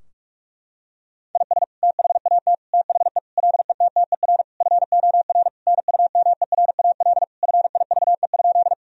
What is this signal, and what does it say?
Morse code, heard as a keyed tone, about 43 wpm, 700 Hz. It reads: IS THAT THE LETTER FOR NUMERAL FIVE?
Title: Sentences from Top 1000 Words - Review Part 2 - 45wpm